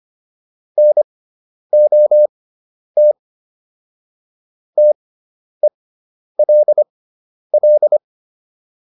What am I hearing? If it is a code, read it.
Morse code, heard as a keyed tone, about 25 wpm, spaced as for 10 wpm, 600 Hz. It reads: NOT TELL
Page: intermediate-practice-files